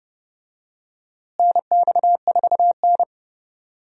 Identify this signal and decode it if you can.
Morse code, heard as a keyed tone, about 30 wpm, 700 Hz. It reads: NX4N
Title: USA Call Signs - 4 Characters - Repeated 3x - 30wpm